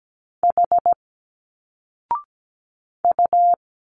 Morse code, heard as a keyed tone, about 17 wpm, 700 Hz. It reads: H U
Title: Single Letters 17wpm